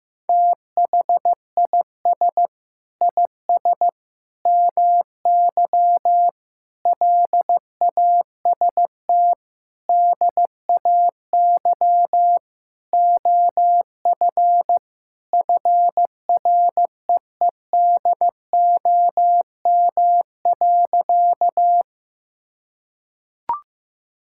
Morse code, heard as a keyed tone, about 15 wpm, 700 Hz. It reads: THIS IS MY LAST DAY OF FREEDOM.
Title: Sentences from Top 1000 Words - Review Part 5 - 15wpm